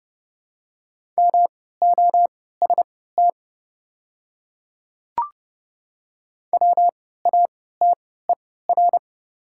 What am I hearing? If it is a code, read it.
Morse code, heard as a keyed tone, about 30 wpm, spaced as for 17 wpm, 690 Hz. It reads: MOST WATER
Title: 3x Character Spacing - Top 500 Words - 30wpm